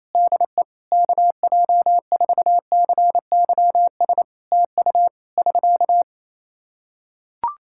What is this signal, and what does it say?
Morse code, heard as a keyed tone, about 27 wpm, 700 Hz. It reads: DE KJ4CYS TU <SK>
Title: Practice QSO - Exchange 4 - Ending - 28wpm